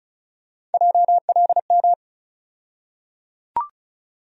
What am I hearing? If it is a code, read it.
Morse code, heard as a keyed tone, about 36 wpm, 700 Hz. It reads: JLM
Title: USA Call Sign Postfixes - 3 Letters 35wpm